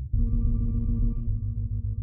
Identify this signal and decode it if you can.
There is no code.